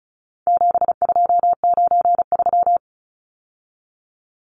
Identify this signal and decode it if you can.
Morse code, heard as a keyed tone, about 34 wpm, 700 Hz. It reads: 7293
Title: Numbers - 4-Digit 35wpm